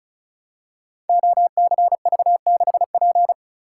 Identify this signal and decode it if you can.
Morse code, heard as a keyed tone, about 36 wpm, 700 Hz. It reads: OCV6P
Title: Code Groups - 5 Letter:Numbers - 35wpm